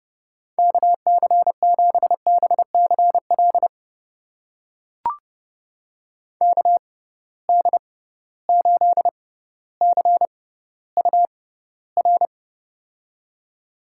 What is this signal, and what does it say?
Morse code, heard as a keyed tone, about 30 wpm, spaced as for 15 wpm, 700 Hz. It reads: KC7BCL KD8CUR
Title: USA Call Signs - 6 Characters - 6x Character Spacing - 30wpm